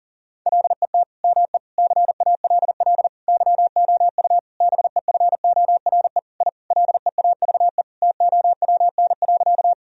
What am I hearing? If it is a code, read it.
Morse code, heard as a keyed tone, about 40 wpm, 700 Hz. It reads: LET ME CALL YOU BEFORE I LEAVE TOWN.
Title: Sentences from Top 1000 Words - Review Part 2 - 40wpm